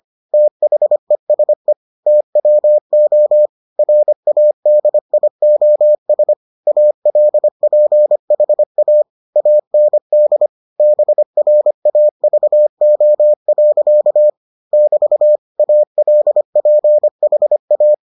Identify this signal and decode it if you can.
Morse code, heard as a keyed tone, about 25 wpm, 600 Hz. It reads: THESE TWO RADIOS ALPHA AND BRAVO. <BT> ALPHA